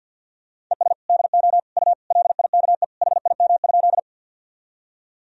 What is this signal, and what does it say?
Morse code, heard as a keyed tone, about 50 wpm, 700 Hz. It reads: ES DO U LIKE HIM?
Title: Sentences from Top 100 Words and Abbreviations 50wpm